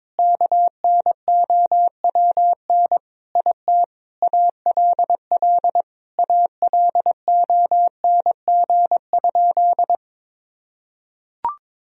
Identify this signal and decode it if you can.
Morse code, heard as a keyed tone, about 21 wpm, 700 Hz. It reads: KNOWN IT ALL ALONG?
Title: Sentences from Top 1000 Words - Review Part 3 - 22wpm